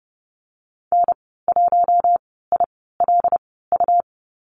Morse code, heard as a keyed tone, about 30 wpm, spaced as for 17 wpm, 700 Hz. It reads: N1ILU
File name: Lesson 19g - Letter w - Call Signs 30wpm